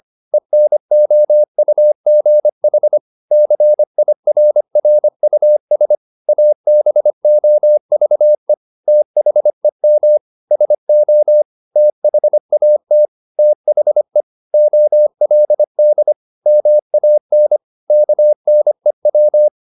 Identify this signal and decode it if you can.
Morse code, heard as a keyed tone, about 25 wpm, 600 Hz. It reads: ENOUGH CIRRUS ABOVE THEM SO THAT THE OLD MAN KNEW